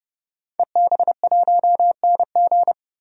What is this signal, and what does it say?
Morse code, heard as a keyed tone, about 30 wpm, 700 Hz. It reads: EB1NG
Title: DXCC-100 Call Signs - 5 Characters - 30wpm